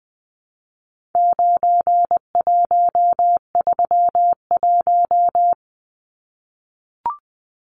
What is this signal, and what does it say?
Morse code, heard as a keyed tone, about 20 wpm, 700 Hz. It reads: 9131
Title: Numbers - 4-Digit 20wpm